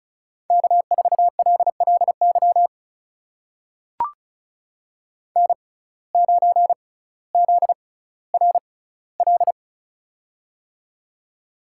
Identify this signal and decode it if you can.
Morse code, heard as a keyed tone, about 36 wpm, spaced as for 18 wpm, 700 Hz. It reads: K4LLY N9ZRL
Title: USA Call Signs - 6x Character Spacing - 35wpm